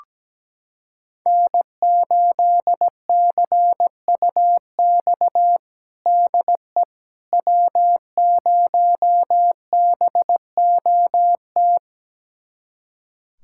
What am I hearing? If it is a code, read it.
Morse code, heard as a keyed tone, about 17 wpm, 700 Hz. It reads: N8CUX DE W0BOT
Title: USA Callsign DE Callsign - 5 Characters - 17wpm